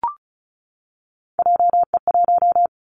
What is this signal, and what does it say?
Morse code, heard as a keyed tone, about 34 wpm, 700 Hz. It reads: JE1
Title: Callsign Prefixes for Top 25 DXCC - 35wpm